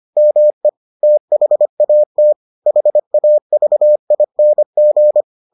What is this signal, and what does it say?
Morse code, heard as a keyed tone, about 25 wpm, 600 Hz. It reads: ME THAT HAVING